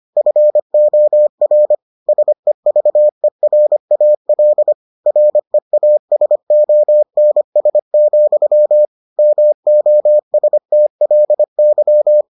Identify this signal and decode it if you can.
Morse code, heard as a keyed tone, about 25 wpm, 600 Hz. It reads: FOR SEVERAL REASONS, MOSTLY